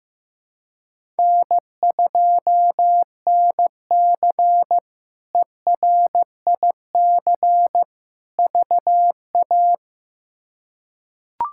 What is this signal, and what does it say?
Morse code, heard as a keyed tone, about 16 wpm, 700 Hz. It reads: N2NC ERIC VA
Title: K1USN SST QSO Practice - 15wpm